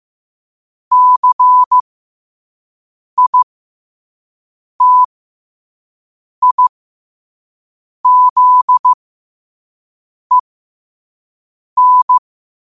Morse code, heard as a keyed tone, about 15 wpm, spaced as for 5 wpm, 1000 Hz. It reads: CITIZEN
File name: qotdfiles / 5wpm